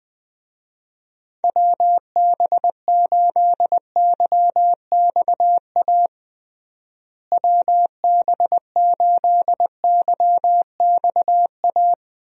Morse code, heard as a keyed tone, about 21 wpm, 700 Hz. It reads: WB8YXA WB8YXA
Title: USA Call Signs - 6 Characters - Repeated 2x - 20wpm